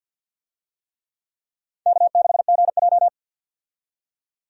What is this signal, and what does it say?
Morse code, heard as a keyed tone, about 51 wpm, 700 Hz. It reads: K6GJ
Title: USA Call Signs - 4 Characters - 50wpm